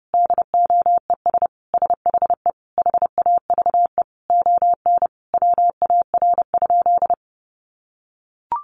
Morse code, heard as a keyed tone, about 30 wpm, 700 Hz. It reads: DOES SHE HAVE ON WAR?
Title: Sentences from Top 1200 Words 30wpm